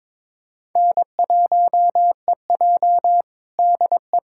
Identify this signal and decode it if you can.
Morse code, heard as a keyed tone, about 22 wpm, 700 Hz. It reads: N1EJ DE
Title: USA Callsign DE Callsign - 4 Characters - 22wpm